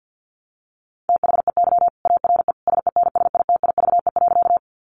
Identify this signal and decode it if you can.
Morse code, heard as a keyed tone, about 48 wpm, 700 Hz. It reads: THEY ARE SENSITIVE.
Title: Sentences from Top 1400 Words 50wpm